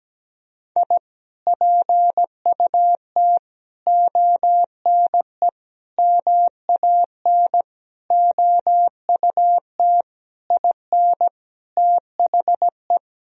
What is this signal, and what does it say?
Morse code, heard as a keyed tone, about 17 wpm, 700 Hz. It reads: I PUT ONE MAN OUT IN THE
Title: Sentences from Top 1000 Words - Review Part 2 - 17wpm